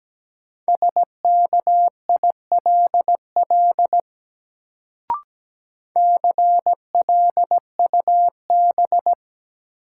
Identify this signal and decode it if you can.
Morse code, heard as a keyed tone, about 17 wpm, 700 Hz. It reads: SKILL CLUB
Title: Sentences from Top 1200 Words - New Word Drill - 17wpm